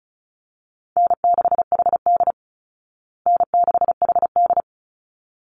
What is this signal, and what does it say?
Morse code, heard as a keyed tone, about 34 wpm, 700 Hz. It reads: N6HD N6HD
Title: Call Signs Repeated 3x - 35wpm